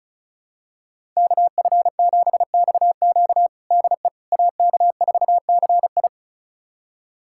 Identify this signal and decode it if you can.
Morse code, heard as a keyed tone, about 36 wpm, 700 Hz. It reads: KF7XQ DE AK4CI
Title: USA Callsign DE Callsign - 5 Characters - 35wpm